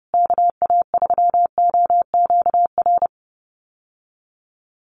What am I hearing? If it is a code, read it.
Morse code, heard as a keyed tone, about 28 wpm, 700 Hz. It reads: KA3OQR
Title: USA Call Signs Repeated 3x 30wpm